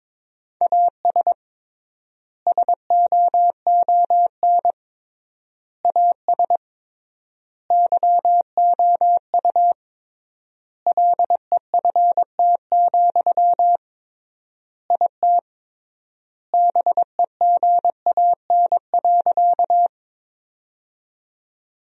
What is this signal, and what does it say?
Morse code, heard as a keyed tone, about 22 wpm, 700 Hz. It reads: AS SOON AS YOU LEFT, IT BEGAN.
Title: 3x Word-Spacing - Sentences from Top 400 Words - 22wpm